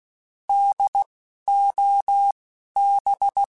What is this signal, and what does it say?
Morse code, heard as a keyed tone, about 16 wpm, 800 Hz. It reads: D O B
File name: Alphabetical / Alphabetical mp3s / Ds @10